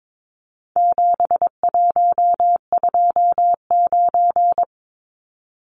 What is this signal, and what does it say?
Morse code, heard as a keyed tone, about 22 wpm, 700 Hz. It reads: 7129
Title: Numbers - 4-Digit 22wpm